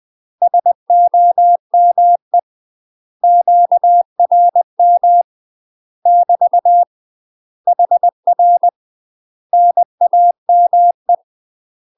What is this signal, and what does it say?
Morse code, heard as a keyed tone, about 20 wpm, 700 Hz. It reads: SOME QRM <BT> HR NAME
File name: UK MASTER QSOs TRAINING / 20WPM / UK MASTER QSOs 0045